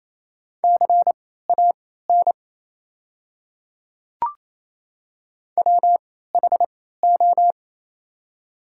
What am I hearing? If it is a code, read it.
Morse code, heard as a keyed tone, about 28 wpm, spaced as for 16 wpm, 700 Hz. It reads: CAN WHO
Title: 3x Character Spacing - Top 500 Words - 28wpm